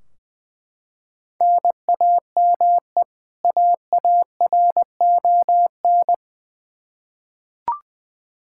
Morse code, heard as a keyed tone, about 20 wpm, 700 Hz. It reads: NAME AARON
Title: Practice QSO - Exchange 1 - Beginning - 20wpm